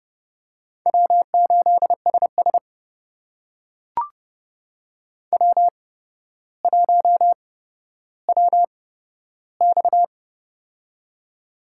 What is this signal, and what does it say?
Morse code, heard as a keyed tone, about 30 wpm, spaced as for 13 wpm, 700 Hz. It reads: W8SS W1WX
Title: USA Call Signs - 4 Characters - 8x Character Spacing - 30wpm